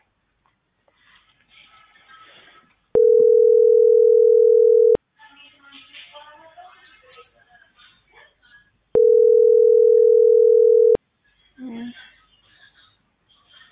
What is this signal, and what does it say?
Ringback tone (North America), 440 + 480 Hz.